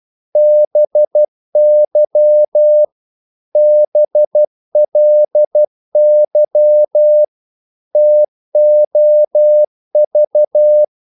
Morse code, heard as a keyed tone, about 12 wpm, 600 Hz. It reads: BY BLY TOV
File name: Korte ord DK 12wpm